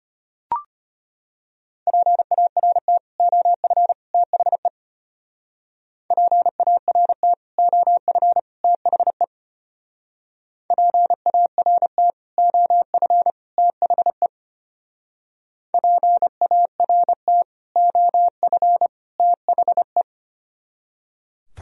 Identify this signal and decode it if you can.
Morse code, heard as a keyed tone, about 31 wpm, 700 Hz. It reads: PART OF THE PART OF THE PART OF THE PART OF THE
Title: Speed-Racing - Sets of 3 Words 25wpm